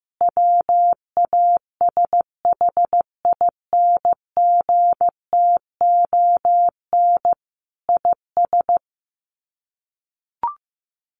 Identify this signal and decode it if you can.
Morse code, heard as a keyed tone, about 15 wpm, 700 Hz. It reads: WASHINGTON IS
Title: Sets of 5 Words Encore2 15wpm